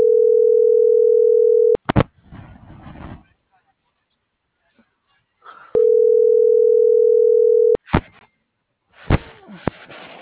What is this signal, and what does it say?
Ringback tone (North America), 440 + 480 Hz.